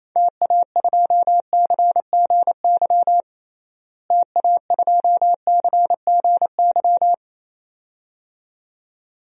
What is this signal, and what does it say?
Morse code, heard as a keyed tone, about 28 wpm, 700 Hz. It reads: TA2CGY TA2CGY
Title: DXCC-100 Call Signs - 6 Characters - Repeated 2x - 28wpm